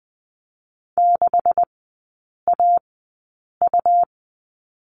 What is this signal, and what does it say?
Morse code, heard as a keyed tone, about 20 wpm, spaced as for 8 wpm, 700 Hz. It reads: 6AU
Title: 178 Lesson 29e - Slash - Three Characters 20wpm